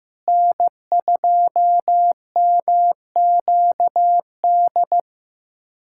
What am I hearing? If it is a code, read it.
Morse code, heard as a keyed tone, about 15 wpm, 700 Hz. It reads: N2MQD
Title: USA Callsign DE Callsign 15wpm